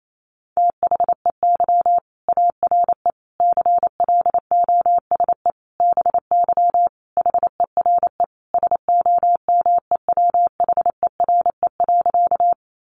Morse code, heard as a keyed tone, about 28 wpm, 700 Hz. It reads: THEY ARE CLOSE BY HERE SOMEWHERE.
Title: Sentences from Top 1400 Words 28wpm